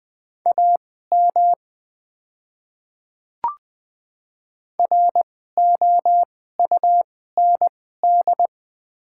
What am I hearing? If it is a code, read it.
Morse code, heard as a keyed tone, about 20 wpm, spaced as for 14 wpm, 700 Hz. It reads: AM ROUND
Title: 2x Character Spacing - Top 500 Words - 20wpm